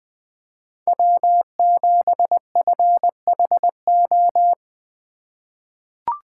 Morse code, heard as a keyed tone, about 20 wpm, 700 Hz. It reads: W7FHO
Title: USA Call Signs - 5 Characters - 2x Character Spacing - 20wpm